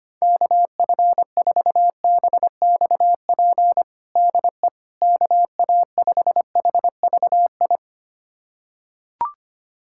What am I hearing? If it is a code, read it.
Morse code, heard as a keyed tone, about 25 wpm, 700 Hz. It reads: KF4BXP DE KA5HVI
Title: USA Callsign DE Callsign - 6 Characters - 25wpm